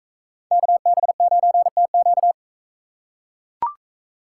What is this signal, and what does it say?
Morse code, heard as a keyed tone, about 40 wpm, 700 Hz. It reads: KB9TQ